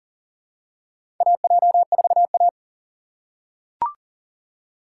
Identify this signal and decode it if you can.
Morse code, heard as a keyed tone, about 37 wpm, 700 Hz. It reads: AJ4A
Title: USA Call Signs - 4 Characters - 3x Character Spacing - 40wpm